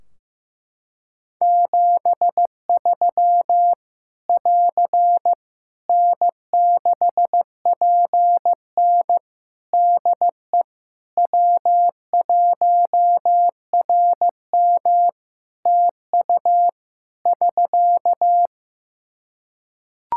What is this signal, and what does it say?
Morse code, heard as a keyed tone, about 15 wpm, 700 Hz. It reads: 73 <AR> N6PN DE W1RM TU <SK>
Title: Practice QSO - Exchange 4 - Ending - 15wpm